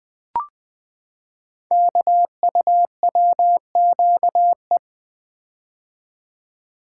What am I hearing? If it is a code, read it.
Morse code, heard as a keyed tone, about 20 wpm, 700 Hz. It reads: KUWQE
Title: Code Groups - 5 Letters - 20wpm